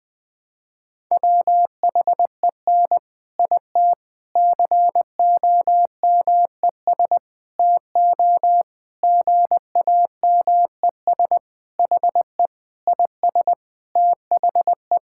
Morse code, heard as a keyed tone, about 20 wpm, 700 Hz. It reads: WHEN IT COMES TO GAMES HE IS THE
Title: Sentences with 3-Word Phrases - Spoken Beginning - 20wpm